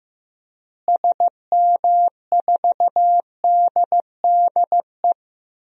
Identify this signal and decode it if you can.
Morse code, heard as a keyed tone, about 15 wpm, 700 Hz. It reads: SM4DDE
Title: DXCC-100 Call Signs - 6 Characters - Repeated 2x - 15wpm